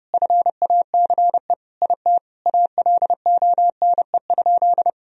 Morse code, heard as a keyed tone, about 32 wpm, 700 Hz. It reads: FACE IT ALONE?
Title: Sentences from Top 1000 Words - Review Part 3 - 30wpm